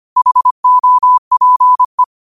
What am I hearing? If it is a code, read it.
Morse code, heard as a keyed tone, about 25 wpm, 1000 Hz. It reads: HOPE
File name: qotdfiles / 25wpm